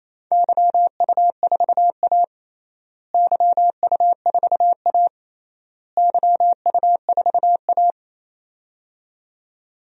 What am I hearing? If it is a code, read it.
Morse code, heard as a keyed tone, about 28 wpm, 700 Hz. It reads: YU4A YU4A YU4A
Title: DXCC-100 Call Signs - 4 Characters - Repeated 3x - 28wpm